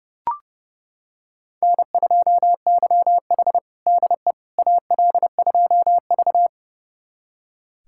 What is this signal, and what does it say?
Morse code, heard as a keyed tone, about 32 wpm, 700 Hz. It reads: N2YH DE AL2V
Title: USA Callsign DE Callsign - 4 Characters - 30wpm